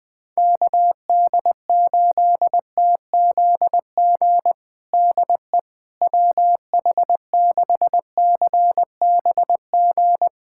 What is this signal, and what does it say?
Morse code, heard as a keyed tone, about 20 wpm, 700 Hz. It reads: KD8TZG DE WH6CBG
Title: USA Callsign DE Callsign - 6 Characters - 20wpm